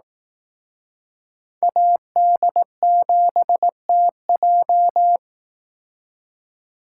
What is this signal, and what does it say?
Morse code, heard as a keyed tone, about 18 wpm, 700 Hz. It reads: AD7TJ
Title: Speed Racing - USA 2x2 Call Signs 15wpm